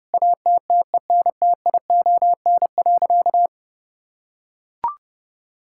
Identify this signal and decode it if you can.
Morse code, heard as a keyed tone, about 32 wpm, 700 Hz. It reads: ATTENTION.
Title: Sentences from Top 1000 Words - Review Part 6 - 30wpm